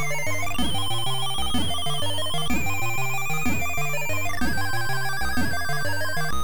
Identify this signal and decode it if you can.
There is no code.